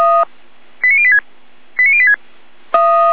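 Five-tone selective call (ZVEI), heard as quick runs of five tones, tones of about 70 ms. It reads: 89086 89086
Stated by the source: ZVEI five-tone selective call received on an SDR.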